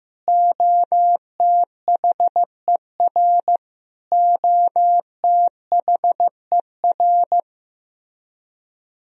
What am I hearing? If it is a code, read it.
Morse code, heard as a keyed tone, about 15 wpm, 700 Hz. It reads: OTHER OTHER
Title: Top 100 Words Repeated 15wpm